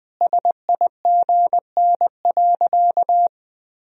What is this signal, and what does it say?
Morse code, heard as a keyed tone, about 20 wpm, 700 Hz. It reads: SIGN.
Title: Sentences from Top 1000 Words - Review Part 2 - 20wpm